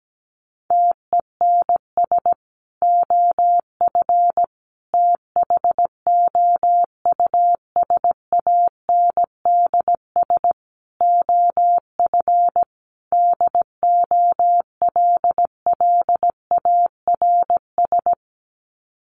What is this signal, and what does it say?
Morse code, heard as a keyed tone, about 17 wpm, 700 Hz. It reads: TENS OF THOUSANDS OF DOLLARS
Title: Sets of 5 Words Encore2 17wpm